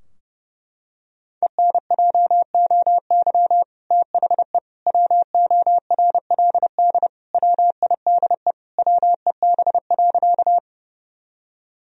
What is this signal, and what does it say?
Morse code, heard as a keyed tone, about 30 wpm, 700 Hz. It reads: ENJOY THE WORLD WIDE WEB.
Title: Sentences from Top 1000 Words - Review Part 4 - 30wpm